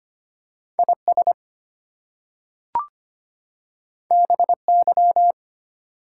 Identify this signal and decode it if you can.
Morse code, heard as a keyed tone, about 25 wpm, 700 Hz. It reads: IS BY
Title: Two Letter Words - 25wpm